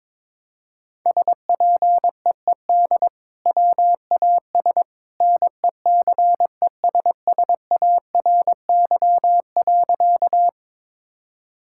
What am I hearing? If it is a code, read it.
Morse code, heard as a keyed tone, about 21 wpm, 700 Hz. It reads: SPEED WAS NECESSARY.
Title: Sentences from Top 1000 Words - Review Part 5 - 22wpm